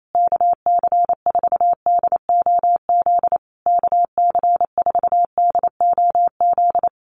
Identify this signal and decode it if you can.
Morse code, heard as a keyed tone, about 28 wpm, 700 Hz. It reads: KC4DOZ KC4DOZ
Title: USA Call Signs Repeated 3x 28wpm